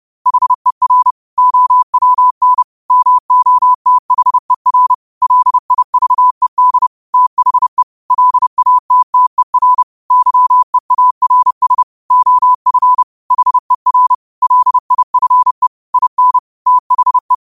Morse code, heard as a keyed tone, about 30 wpm, 1000 Hz. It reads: HER OWN MOTHER LIVED THE LATTER YEARS OF HER LIFE IN THE
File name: qotdfiles / 30wpm